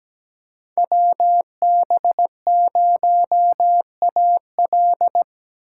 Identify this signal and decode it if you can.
Morse code, heard as a keyed tone, about 17 wpm, 700 Hz. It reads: WB0AL
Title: USA Call Signs - 5 Characters - 4x Character Spacing - 17wpm